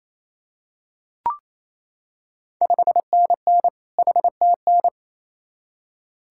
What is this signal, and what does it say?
Morse code, heard as a keyed tone, about 28 wpm, 700 Hz. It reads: 5NN HTN
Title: Ontario QSO Party Exchange 28wpm